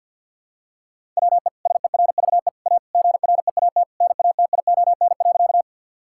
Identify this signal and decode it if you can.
Morse code, heard as a keyed tone, about 50 wpm, 700 Hz. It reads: WE SERVE A GREAT NATION.
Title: Sentences from Top 1000 Words - Review Part 6 - 50wpm